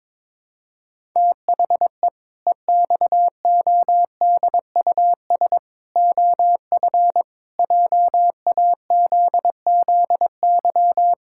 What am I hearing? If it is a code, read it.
Morse code, heard as a keyed tone, about 22 wpm, 700 Hz. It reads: THE EXODUS OF JAZZY
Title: Pangram Challenge v2 - 22wpm